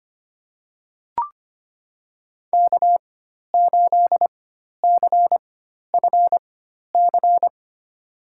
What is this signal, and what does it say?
Morse code, heard as a keyed tone, about 25 wpm, spaced as for 12 wpm, 700 Hz. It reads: K8CFC
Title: USA Call Signs - 5 Characters - 4x Character Spacing - 25wpm